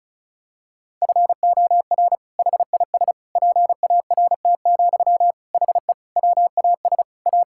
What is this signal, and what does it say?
Morse code, heard as a keyed tone, about 36 wpm, 700 Hz. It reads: FOR HIS PART, HE WAS A
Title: Sentences from Top 1000 Words - Review Part 5 - 35wpm